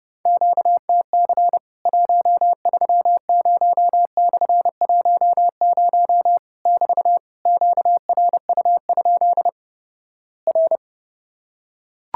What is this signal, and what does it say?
Morse code, heard as a keyed tone, about 31 wpm, 700 Hz. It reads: QTC 130/10 <BT> QRU? R
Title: Worked All Europe DX Contest - QTC Exchange - 30wpm